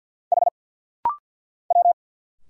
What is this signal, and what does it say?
Morse code, heard as a keyed tone, about 50 wpm, 690 Hz. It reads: H W
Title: Single Letter-Number Alt - Rapid-Fire 50wpm